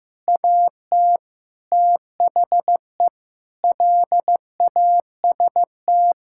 Morse code, heard as a keyed tone, about 15 wpm, 700 Hz. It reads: AT THE LAST
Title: Sentences from Top 1600 Words 15wpm